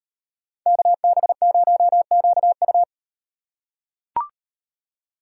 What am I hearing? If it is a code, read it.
Morse code, heard as a keyed tone, about 38 wpm, 700 Hz. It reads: KB0QU